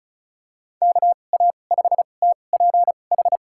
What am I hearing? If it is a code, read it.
Morse code, heard as a keyed tone, about 34 wpm, spaced as for 25 wpm, 700 Hz. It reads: KA5TPH
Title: USA Call Signs - 2x Character Spacing - 35wpm